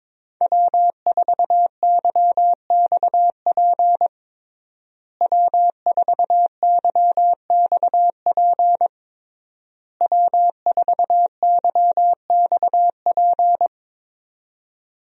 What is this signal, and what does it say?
Morse code, heard as a keyed tone, about 23 wpm, 700 Hz. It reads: W4YXP W4YXP W4YXP
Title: USA Call Signs - 5 Characters - Repeated 3x - 22wpm